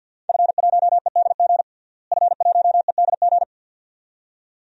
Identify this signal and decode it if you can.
Morse code, heard as a keyed tone, about 53 wpm, 700 Hz. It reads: F1EDG F1EDG
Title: DXCC-100 Call Signs - 5 Characters - Repeated 2x - 50wpm